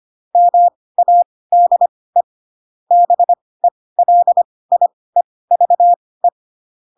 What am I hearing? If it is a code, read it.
Morse code, heard as a keyed tone, about 25 wpm, spaced as for 18 wpm, 700 Hz. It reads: MADE BELIEVE
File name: Pinocchio ch3 12 25 ik1hge